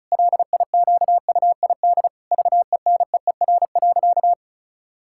Morse code, heard as a keyed tone, about 36 wpm, 700 Hz. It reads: LIQUID VENEER.
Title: Pangram Challenge v2 - 35wpm